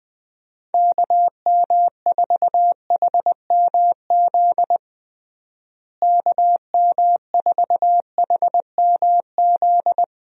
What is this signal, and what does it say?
Morse code, heard as a keyed tone, about 20 wpm, 700 Hz. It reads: KM4HMZ KM4HMZ
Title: USA Call Signs - 6 Characters - Repeated 3x - 20wpm